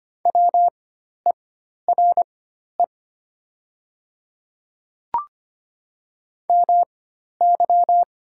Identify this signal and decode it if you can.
Morse code, heard as a keyed tone, about 25 wpm, spaced as for 12 wpm, 700 Hz. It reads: WERE MY
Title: 4x Character Spacing - Top 100 Words - 25wpm